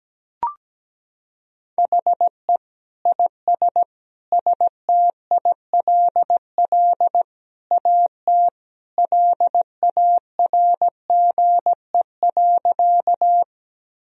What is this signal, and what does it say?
Morse code, heard as a keyed tone, about 17 wpm, 700 Hz. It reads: HE IS STILL AT LARGE.
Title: Sentences from Top 1000 Words - Review Part 1 - 17wpm